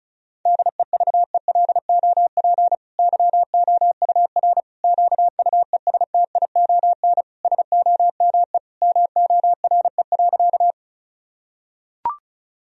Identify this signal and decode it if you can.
Morse code, heard as a keyed tone, about 36 wpm, 700 Hz. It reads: DEVELOP YOUR QUESTION SOME MORE.
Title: Sentences from Top 1000 Words - Review Part 2 - 35wpm